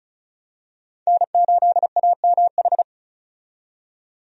Morse code, heard as a keyed tone, about 34 wpm, 700 Hz. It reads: N8AMH
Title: USA Call Signs - 5 Characters - 35wpm